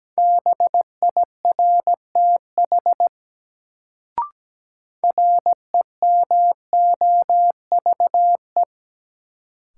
Morse code, heard as a keyed tone, about 17 wpm, 700 Hz. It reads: BIRTH REMOVE
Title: Sentences from Top 800 Words - New Word Drill - 17wpm